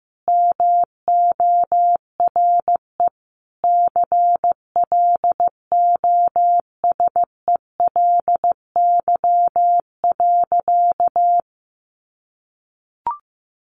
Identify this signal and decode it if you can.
Morse code, heard as a keyed tone, about 15 wpm, 700 Hz. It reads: MORE CLOSELY.
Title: Sentences from Top 1400 Words 15wpm